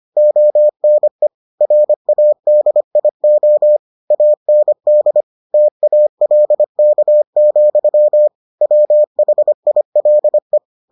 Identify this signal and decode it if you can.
Morse code, heard as a keyed tone, about 25 wpm, 600 Hz. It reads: ONE RADIO AND TALK, WHILE